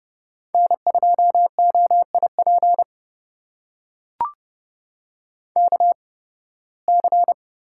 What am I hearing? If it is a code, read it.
Morse code, heard as a keyed tone, about 30 wpm, spaced as for 20 wpm, 700 Hz. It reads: N2OIP KC
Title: USA Call Signs - 8x Character Spacing - 30wpm